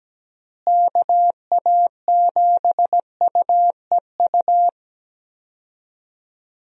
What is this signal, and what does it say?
Morse code, heard as a keyed tone, about 17 wpm, 700 Hz. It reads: KA7UEU
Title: USA Call Signs - 6 Characters - Repeated 2x - 17wpm